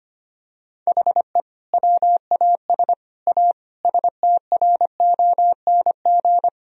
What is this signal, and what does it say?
Morse code, heard as a keyed tone, about 25 wpm, 700 Hz. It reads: HE WAS A STRONG
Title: Sentences from Top 1000 Words - Review Part 3 - 25wpm